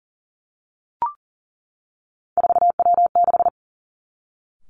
Morse code, heard as a keyed tone, about 41 wpm, 700 Hz. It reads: 4W6
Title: Three Challenging Letter-Number Combinations 40wpm